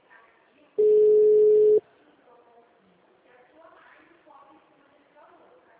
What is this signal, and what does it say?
Ringback tone (Europe, 425 Hz), with speech.